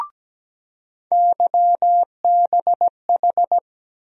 Morse code, heard as a keyed tone, about 17 wpm, 700 Hz. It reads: YBH
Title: USA Call Sign Postfixes - 3 Letters 17wpm